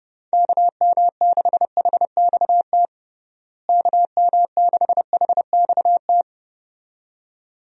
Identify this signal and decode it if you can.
Morse code, heard as a keyed tone, about 30 wpm, 700 Hz. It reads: KM6HXT KM6HXT
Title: USA Call Signs - 6 Characters - Repeated 2x - 30wpm